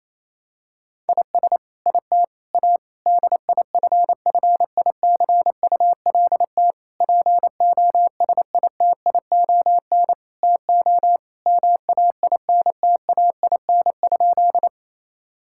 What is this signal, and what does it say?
Morse code, heard as a keyed tone, about 28 wpm, 700 Hz. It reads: IS IT A DIFFICULT POSITION TO MAINTAIN?
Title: Sentences from Top 1000 Words - Review Part 5 - 28wpm